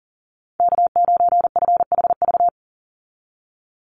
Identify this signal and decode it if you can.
Morse code, heard as a keyed tone, about 40 wpm, 700 Hz. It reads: K9FHV
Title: USA Call Signs 40wpm